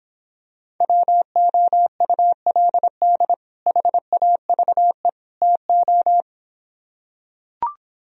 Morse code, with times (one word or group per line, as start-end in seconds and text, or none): 0.8-3.3 WOULD
3.7-5.1 HAVE
5.4-6.2 TO